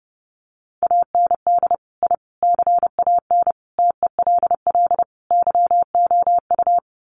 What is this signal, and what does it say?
Morse code, heard as a keyed tone, about 30 wpm, 700 Hz. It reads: AND I CAN TELL YOU
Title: Sets of 5 Words Encore2 30wpm